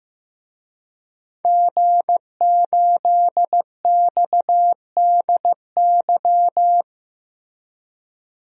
Morse code, heard as a keyed tone, about 15 wpm, 700 Hz. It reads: G8XDY
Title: DXCC-100 Call Signs - 5 Characters - Repeated 3x - 15wpm